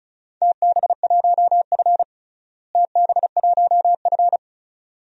Morse code, heard as a keyed tone, about 34 wpm, 700 Hz. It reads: TB1F TB1F
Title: DXCC-100 Call Signs - 4 Characters - Repeated 2x - 35wpm